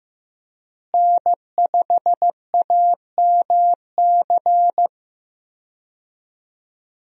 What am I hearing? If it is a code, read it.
Morse code, heard as a keyed tone, about 15 wpm, 700 Hz. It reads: N5AMC
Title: USA Call Signs - 5 Characters - Repeated 2x - 15wpm